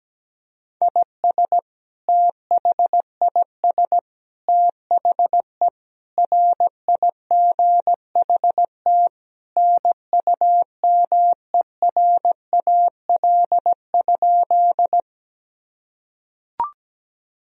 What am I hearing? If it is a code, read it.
Morse code, heard as a keyed tone, about 17 wpm, 700 Hz. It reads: IS THIS THE RIGHT NUMERAL?
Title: Sentences from Top 1000 Words - Review Part 2 - 17wpm